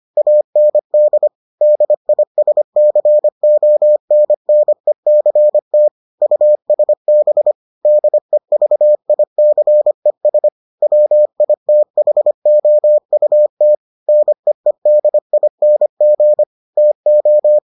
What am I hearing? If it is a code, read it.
Morse code, heard as a keyed tone, about 25 wpm, 600 Hz. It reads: AND DISCONNECT USB DEVICES WITHOUT NEEDING TO